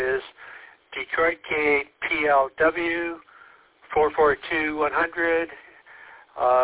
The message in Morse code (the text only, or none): none